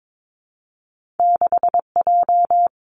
Morse code, heard as a keyed tone, about 22 wpm, 700 Hz. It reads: 6J
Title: Two Challenging Letter-Number Combinations 22wpm